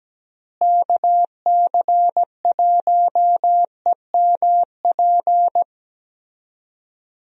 Morse code, heard as a keyed tone, about 17 wpm, 700 Hz. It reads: KC1EMP
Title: USA Call Signs - 6 Characters - Repeated 2x - 17wpm